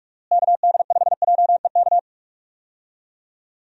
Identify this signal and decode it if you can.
Morse code, heard as a keyed tone, about 46 wpm, 700 Hz. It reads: KD5JEK
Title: USA Call Signs - 6 Characters - Repeated 3x - 45wpm